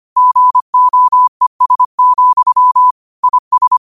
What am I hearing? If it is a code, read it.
Morse code, heard as a keyed tone, about 25 wpm, 1000 Hz. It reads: GOES, IS
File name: qotdfiles / 25wpm